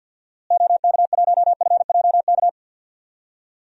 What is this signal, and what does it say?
Morse code, heard as a keyed tone, about 51 wpm, 700 Hz. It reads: OK1FJK
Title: DXCC-100 Call Signs - 6 Characters - Repeated 3x - 50wpm